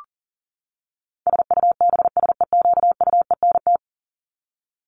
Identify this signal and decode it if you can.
Morse code, heard as a keyed tone, about 40 wpm, 700 Hz. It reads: SUBSEQUENT
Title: LY Suffix - Top 100 Words with Base Words 40wpm